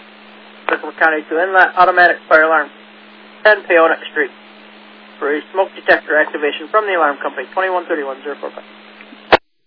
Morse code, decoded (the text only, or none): none